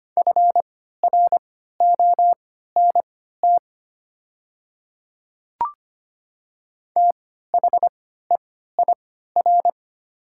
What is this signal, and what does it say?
Morse code, heard as a keyed tone, about 25 wpm, spaced as for 14 wpm, 700 Hz. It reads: FRONT THEIR
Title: 3x Character Spacing - Top 500 Words - 25wpm